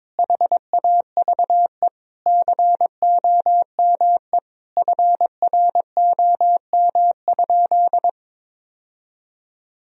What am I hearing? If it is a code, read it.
Morse code, heard as a keyed tone, about 22 wpm, 700 Hz. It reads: HAVE COME FROM?
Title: Sentences from Top 1000 Words - Review Part 2 - 22wpm